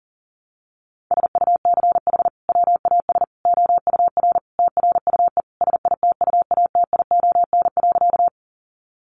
Morse code, heard as a keyed tone, about 40 wpm, 700 Hz. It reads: SUCH WAS OUR TRUE SITUATION.
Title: Sentences from Top 1200 Words 40wpm